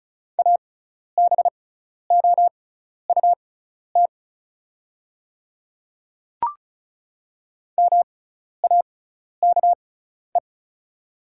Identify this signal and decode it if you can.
Morse code, heard as a keyed tone, about 36 wpm, spaced as for 12 wpm, 690 Hz. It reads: ABOUT MAKE
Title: 6x Character Spacing - Top 200 Words - 35wpm